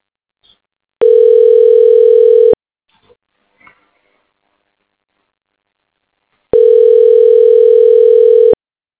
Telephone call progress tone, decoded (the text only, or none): ringback tone